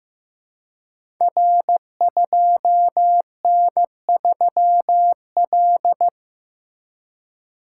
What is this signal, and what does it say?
Morse code, heard as a keyed tone, about 15 wpm, 700 Hz. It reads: R2N3L
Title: Code Groups - 5 Letter:Numbers - 15wpm